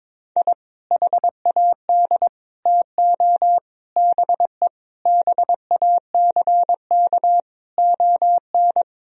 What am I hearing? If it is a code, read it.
Morse code, heard as a keyed tone, about 22 wpm, 700 Hz. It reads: I HAD TO BE BACK ON
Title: Sentences from Top 1000 Words - Review Part 4 - 22wpm